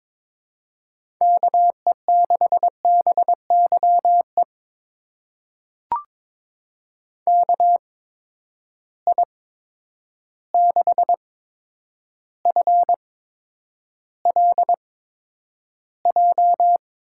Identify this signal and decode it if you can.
Morse code, heard as a keyed tone, about 22 wpm, spaced as for 9 wpm, 700 Hz. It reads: KE6BYE KI6FLJ
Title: USA Call Signs - 6 Characters - 8x Character Spacing - 22wpm